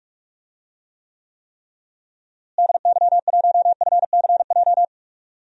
Morse code, heard as a keyed tone, about 46 wpm, 700 Hz. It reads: DY1FCJ
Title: DXCC-100 Call Signs - 6 Characters - Repeated 3x - 45wpm